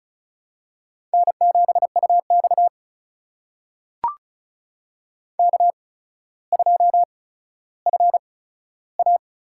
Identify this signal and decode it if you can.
Morse code, heard as a keyed tone, about 36 wpm, spaced as for 15 wpm, 700 Hz. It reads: N7UX K2FA
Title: USA Call Signs - 4 Characters - 8x Character Spacing - 35wpm